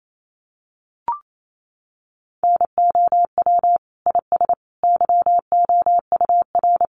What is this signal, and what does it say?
Morse code, heard as a keyed tone, about 28 wpm, 700 Hz. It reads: NOW IS YOUR
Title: Sentences from Top 100 Words 28wpm